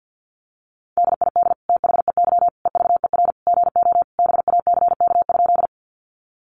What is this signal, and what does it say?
Morse code, heard as a keyed tone, about 50 wpm, 700 Hz. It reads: DID THEY EVER GO BACK?
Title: Sentences from Top 400 Words 50wpm